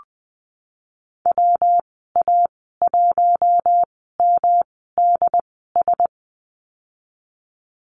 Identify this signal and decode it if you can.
Morse code, heard as a keyed tone, about 20 wpm, spaced as for 15 wpm, 700 Hz. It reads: WA1MDS
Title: 128 Lesson 21g - Number 6 - Call Signs 20wpm